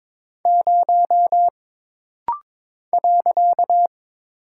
Morse code, heard as a keyed tone, about 22 wpm, 700 Hz. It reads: 0 .
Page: This rapid-fire timing is geared to push your ICR (Instant Character Recognition) skills.